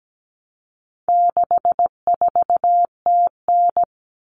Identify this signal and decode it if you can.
Morse code, heard as a keyed tone, about 17 wpm, 700 Hz. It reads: 64TN
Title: All Cut Numbers - 4-Digit 17wpm